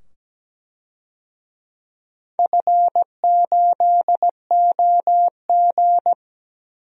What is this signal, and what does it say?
Morse code, heard as a keyed tone, about 17 wpm, 700 Hz. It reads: F8OG
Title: DXCC-100 Call Signs - 4 Characters - 17wpm